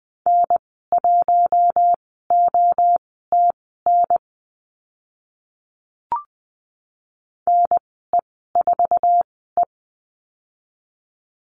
Morse code, heard as a keyed tone, about 20 wpm, spaced as for 15 wpm, 700 Hz. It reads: N1OTN NE4E
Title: 062 Lesson 11g - Letter h - Call Signs 20wpm